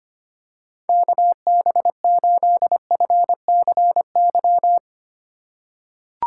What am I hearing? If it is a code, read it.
Morse code, heard as a keyed tone, about 25 wpm, 700 Hz. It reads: KB8FCY
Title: USA Call Signs - 6 Characters - 3x Character Spacing - 25wpm